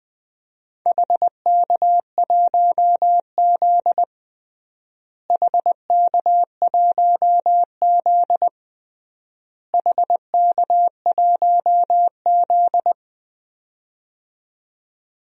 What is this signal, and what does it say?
Morse code, heard as a keyed tone, about 20 wpm, 700 Hz. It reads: HK1Z HK1Z HK1Z
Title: DXCC-100 Call Signs - 4 Characters - Repeated 3x - 20wpm